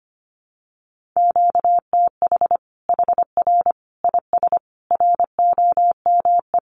Morse code, heard as a keyed tone, about 25 wpm, 700 Hz. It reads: QTH HR IS ROME
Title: Common QSO Phrases 25wpm